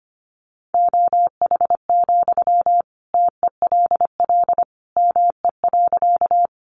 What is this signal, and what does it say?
Morse code, heard as a keyed tone, about 25 wpm, 700 Hz. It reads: OH, TELL ME.
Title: Sentences from Top 500 Words 25wpm